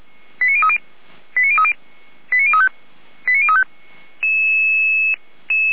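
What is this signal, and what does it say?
Five-tone selective call (ZVEI), heard as quick runs of five tones, tones of about 80 ms. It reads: 89030 89030 89035 89035
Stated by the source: ZVEI five-tone call 89050 received by SDR.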